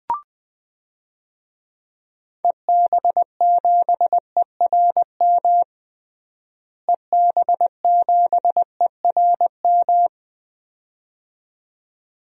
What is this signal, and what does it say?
Morse code, heard as a keyed tone, about 20 wpm, 700 Hz. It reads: EB7ERM EB7ERM
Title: DXCC-100 Call Signs - 6 Characters - Repeated 2x - 20wpm